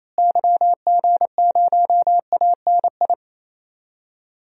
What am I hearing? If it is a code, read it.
Morse code, heard as a keyed tone, about 27 wpm, 700 Hz. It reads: YG0ANI
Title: DXCC-100 Call Signs - 6 Characters - Repeated 3x - 28wpm